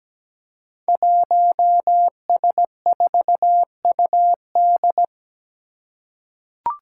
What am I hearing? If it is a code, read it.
Morse code, heard as a keyed tone, about 17 wpm, 700 Hz. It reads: 1S4UD
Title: Five Challenging Letter-Number Combinations 17wpm